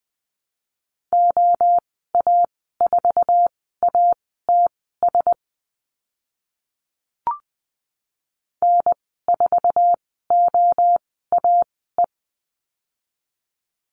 Morse code, heard as a keyed tone, about 20 wpm, spaced as for 14 wpm, 700 Hz. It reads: OA4ATS N4OAE
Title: 076 Lesson 13g - Letter L - Call Signs 20wpm